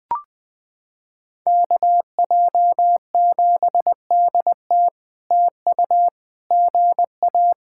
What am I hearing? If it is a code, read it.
Morse code, heard as a keyed tone, about 20 wpm, 700 Hz. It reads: KJ7DT TU GA
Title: POTA QSO Practice - 20wpm